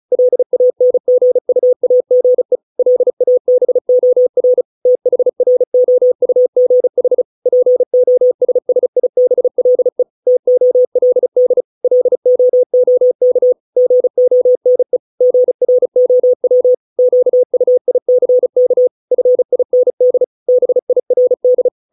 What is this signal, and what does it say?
Morse code, heard as a keyed tone, about 34 wpm, 500 Hz. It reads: LANGUAGE LABOR THROUGH POSSIBLE TOLD LOOK GONE GROW QUICK FIND BIRD